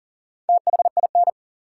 Morse code, heard as a keyed tone, about 40 wpm, 700 Hz. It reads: THIN
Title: Sentences from Top 1200 Words - New Word Drill - 40wpm